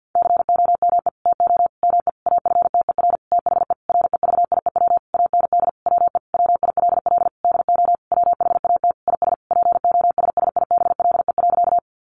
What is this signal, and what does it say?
Morse code, heard as a keyed tone, about 50 wpm, 700 Hz. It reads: COME TO ME AFTER THE REVIEW AND WE WILL DO WHAT IS POSSIBLE.
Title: Sentences from Top 900 Words 50wpm